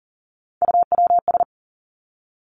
Morse code, heard as a keyed tone, about 40 wpm, 700 Hz. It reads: UWS
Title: Three Challenging Letter-Number Combinations 40wpm